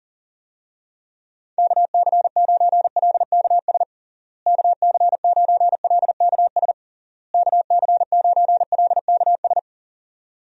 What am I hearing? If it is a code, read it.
Morse code, heard as a keyed tone, about 37 wpm, 700 Hz. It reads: KC9LKS KC9LKS KC9LKS
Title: USA Call Signs - 6 Characters - Repeated 3x - 40wpm